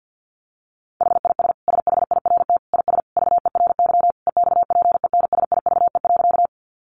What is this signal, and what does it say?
Morse code, heard as a keyed tone, about 50 wpm, 700 Hz. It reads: HIS SHIRT IS VERY EXPENSIVE.
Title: Additional Easy Phrases 50wpm